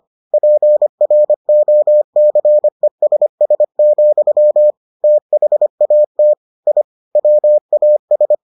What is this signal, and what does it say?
Morse code, heard as a keyed tone, about 25 wpm, 600 Hz. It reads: PROCESS, THAT I WAS